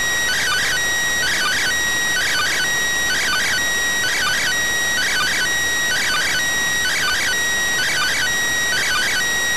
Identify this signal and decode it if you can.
Two-tone teleprinter signal, about 400 Hz shift.